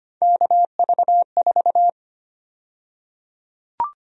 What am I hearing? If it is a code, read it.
Morse code, heard as a keyed tone, about 25 wpm, 700 Hz. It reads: KV4
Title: USA Call Sign Prefixes - Repeated 2x - 25wpm